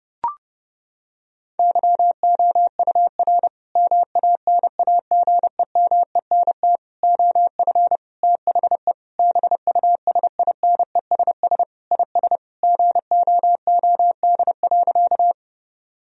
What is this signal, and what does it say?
Morse code, heard as a keyed tone, about 30 wpm, 700 Hz. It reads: YOUR MANAGEMENT OF THE BUSINESS IS GOOD.
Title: Sentences from Top 1000 Words - Review Part 4 - 30wpm